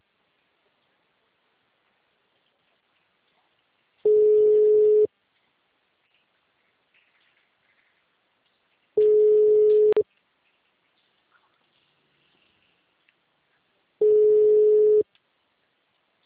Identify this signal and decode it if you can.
Ringback tone (Europe, 425 Hz).